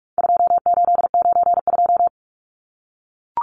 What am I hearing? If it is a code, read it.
Morse code, heard as a keyed tone, about 44 wpm, 700 Hz. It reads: 2892
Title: Numbers - 4-Digit 45wpm